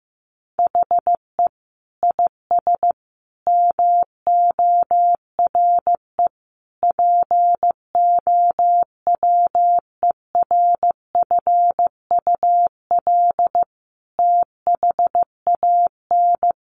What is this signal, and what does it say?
Morse code, heard as a keyed tone, about 15 wpm, 700 Hz. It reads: HE IS MORE POWERFUL THAN
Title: Sentences from Top 1300 Words 15wpm